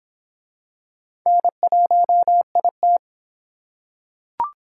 Morse code, heard as a keyed tone, about 23 wpm, 700 Hz. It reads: N1IT